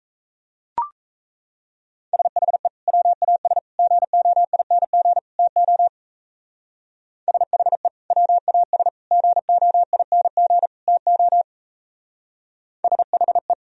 Morse code, heard as a keyed tone, about 32 wpm, 700 Hz. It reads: SHE WAS GOING TO SHE WAS GOING TO SHE
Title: Speed-Racing - Sets of 4 Words 28wpm